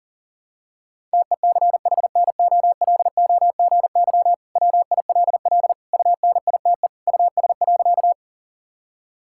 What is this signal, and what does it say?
Morse code, heard as a keyed tone, about 40 wpm, 700 Hz. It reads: TECHNOLOGY WILL UNITE US.
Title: Sentences from Top 1000 Words - Review Part 5 - 40wpm